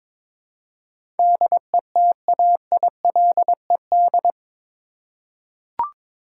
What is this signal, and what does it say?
Morse code, heard as a keyed tone, about 22 wpm, 700 Hz. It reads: DETAILED
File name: practice / 22wpm / Eight Letter Words - 22wpm